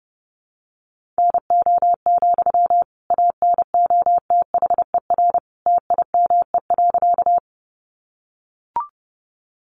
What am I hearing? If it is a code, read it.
Morse code, heard as a keyed tone, about 28 wpm, 700 Hz. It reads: NO, ANOTHER TIME.
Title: Sentences from Top 600 Words 30wpm